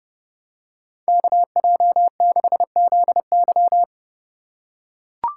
Morse code, heard as a keyed tone, about 30 wpm, 700 Hz. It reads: KJ6ZY
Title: USA Call Signs - 5 Characters - 2x Character Spacing - 30wpm